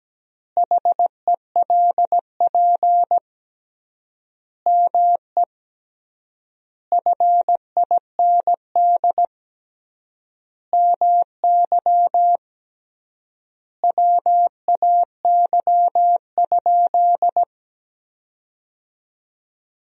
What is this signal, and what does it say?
Morse code, heard as a keyed tone, about 17 wpm, 700 Hz. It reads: HELP ME FIND MY WAY?
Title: 3x Word-Spacing - Sentences from Top 200 Words - 17wpm